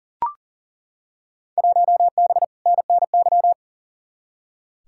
Morse code, heard as a keyed tone, about 40 wpm, 700 Hz. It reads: 1B NNY
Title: Field Day Exchanges Encore - 40wpm